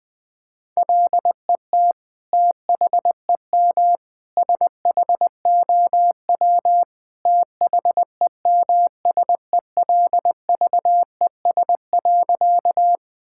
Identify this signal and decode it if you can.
Morse code, heard as a keyed tone, about 20 wpm, 700 Hz. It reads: LET THEM SHOW THEMSELVES.
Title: Sentences from Top 1000 Words - Review Part 5 - 20wpm